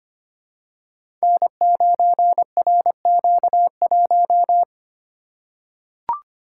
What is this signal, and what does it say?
Morse code, heard as a keyed tone, about 26 wpm, 700 Hz. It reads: N9RQ1
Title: Code Groups - 5 Letter:Numbers - 25wpm